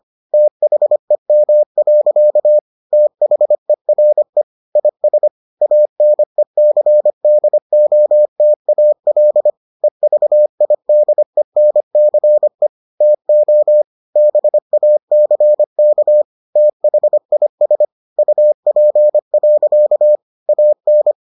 Morse code, heard as a keyed tone, about 25 wpm, 600 Hz. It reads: THEM. THERE IS ANECDOTAL EVIDENCE TO BACK THIS UP. AN